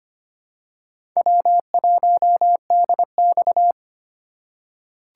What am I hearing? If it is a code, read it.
Morse code, heard as a keyed tone, about 25 wpm, 700 Hz. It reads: W1DX
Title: K1USN SST QSO Practice - 25wpm